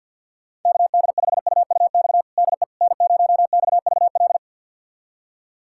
Morse code, heard as a keyed tone, about 52 wpm, 700 Hz. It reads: KD5UUX DE N0XVL
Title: USA Callsign DE Callsign 50wpm